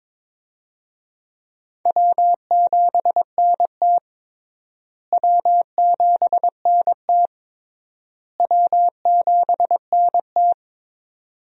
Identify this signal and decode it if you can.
Morse code, heard as a keyed tone, about 22 wpm, 700 Hz. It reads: W7NT W7NT W7NT
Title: USA Call Signs - 4 Characters - Repeated 3x - 22wpm